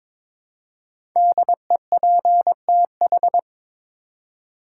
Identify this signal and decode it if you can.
Morse code, heard as a keyed tone, about 22 wpm, 700 Hz. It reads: DEPTH
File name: 5 Letter Words - First 3 Letters Given - 22wpm